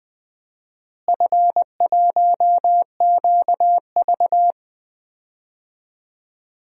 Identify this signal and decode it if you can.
Morse code, heard as a keyed tone, about 20 wpm, 700 Hz. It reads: F1QV
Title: DXCC-100 Call Signs - 4 Characters - Repeated 2x - 20wpm